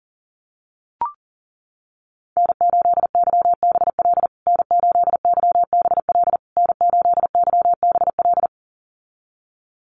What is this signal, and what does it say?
Morse code, heard as a keyed tone, about 40 wpm, 700 Hz. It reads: N8YBL N8YBL N8YBL
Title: USA Call Signs Repeated 3x 40wpm